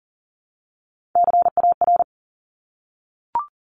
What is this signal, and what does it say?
Morse code, heard as a keyed tone, about 40 wpm, 700 Hz. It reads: CAR
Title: Top 500 Words Repeated 40wpm